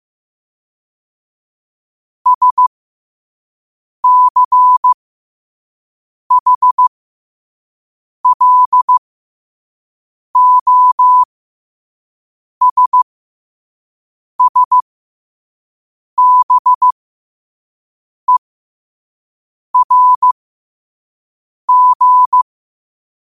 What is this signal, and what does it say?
Morse code, heard as a keyed tone, about 15 wpm, spaced as for 5 wpm, 1000 Hz. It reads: SCHLOSSBERG
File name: qotdfiles / 5wpm